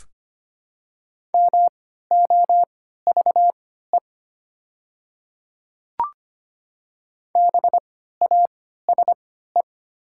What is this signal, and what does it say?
Morse code, heard as a keyed tone, about 25 wpm, spaced as for 14 wpm, 690 Hz. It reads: MOVE BASE
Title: 3x Character Spacing - Top 300 Words - 25wpm